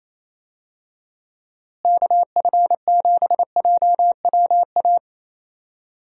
Morse code, heard as a keyed tone, about 28 wpm, 700 Hz. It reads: KF7JWA
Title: USA Call Signs - 6 Characters - Repeated 2x - 28wpm